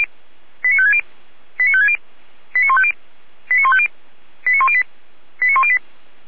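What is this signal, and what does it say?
Five-tone selective call (ZVEI), heard as quick runs of five tones, tones of about 70 ms. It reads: 89560 89560 89150 89150 89108 89108
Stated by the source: ZVEI five-tone selective call received by SDR.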